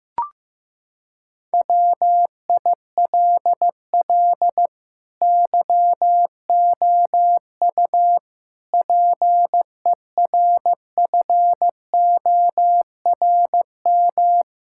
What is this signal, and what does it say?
Morse code, heard as a keyed tone, about 15 wpm, 700 Hz. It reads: WILL YOU PERFORM
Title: Sentences from Top 1000 Words - Review Part 5 - 15wpm